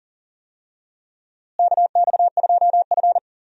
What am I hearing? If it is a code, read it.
Morse code, heard as a keyed tone, about 40 wpm, 700 Hz. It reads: KX2F
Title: USA Call Signs - 4 Characters - Repeated 2x - 40wpm